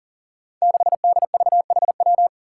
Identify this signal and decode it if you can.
Morse code, heard as a keyed tone, about 37 wpm, 700 Hz. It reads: 6DVHW
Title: Five Challenging Letter-Number Combinations 40wpm